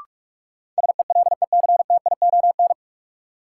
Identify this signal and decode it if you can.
Morse code, heard as a keyed tone, about 44 wpm, 700 Hz. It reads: SELECTION
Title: Sentences from Top 1400 Words - New Word Drill Alt - 45wpm